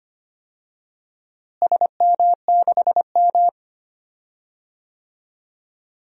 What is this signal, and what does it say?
Morse code, heard as a keyed tone, about 25 wpm, 700 Hz. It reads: SM6M
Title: DXCC-100 Call Signs - 4 Characters - Repeated 2x - 25wpm